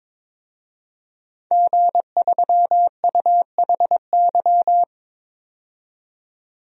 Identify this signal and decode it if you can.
Morse code, heard as a keyed tone, about 22 wpm, 700 Hz. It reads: G3UHY
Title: DXCC-100 Call Signs - 5 Characters - 22wpm